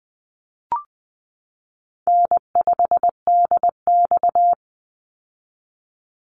Morse code, heard as a keyed tone, about 20 wpm, 700 Hz. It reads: N5DX
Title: Call Signs Repeated 3x - 20wpm